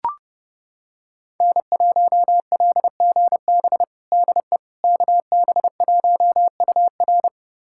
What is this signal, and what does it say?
Morse code, heard as a keyed tone, about 30 wpm, 700 Hz. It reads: N1LGB DE KB1UR
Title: USA Callsign DE Callsign - 5 Characters - 30wpm